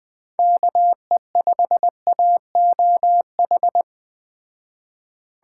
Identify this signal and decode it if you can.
Morse code, heard as a keyed tone, about 20 wpm, 700 Hz. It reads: KE5AOH
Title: Speed Racing - USA 2x3 Call Signs 17wpm